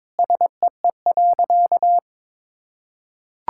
Morse code, heard as a keyed tone, about 22 wpm, 700 Hz. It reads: SEE.
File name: practice / 22wpm / Sentences with 3-Word Phrases - 22wpm